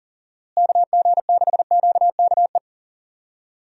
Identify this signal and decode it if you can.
Morse code, heard as a keyed tone, about 40 wpm, 700 Hz. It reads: KG6QKE
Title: USA Call Signs - 6 Characters - Repeated 2x - 40wpm